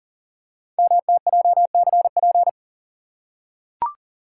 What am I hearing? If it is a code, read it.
Morse code, heard as a keyed tone, about 42 wpm, 700 Hz. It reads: MTJCP
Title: Code Groups - 5 Letters - 40wpm